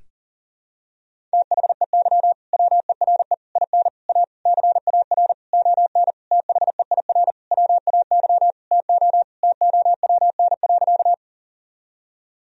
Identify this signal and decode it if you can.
Morse code, heard as a keyed tone, about 40 wpm, 700 Hz. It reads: THEY WERE IN A CAR ON THEIR WAY TO TOWN.
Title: Sentences from Top 1000 Words - Review Part 1 - 40wpm